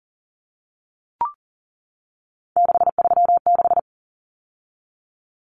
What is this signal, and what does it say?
Morse code, heard as a keyed tone, about 40 wpm, 700 Hz. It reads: 636
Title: Common Cut Numbers - 3-Digit 40wpm